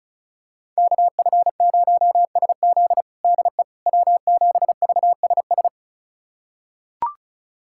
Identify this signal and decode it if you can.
Morse code, heard as a keyed tone, about 36 wpm, 700 Hz. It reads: KF0SZ DE W7VSS
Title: USA Callsign DE Callsign - 5 Characters - 35wpm